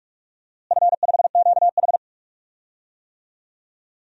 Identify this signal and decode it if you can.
Morse code, heard as a keyed tone, about 45 wpm, 700 Hz. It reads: F5QH
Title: DXCC-100 Call Signs - 4 Characters - Repeated 3x - 45wpm